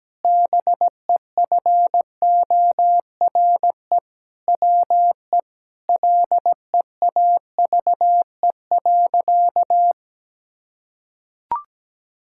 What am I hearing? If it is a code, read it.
Morse code, heard as a keyed tone, about 17 wpm, 700 Hz. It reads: BEFORE WE LEAVE.
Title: Sentences from Top 1000 Words - Review Part 5 - 17wpm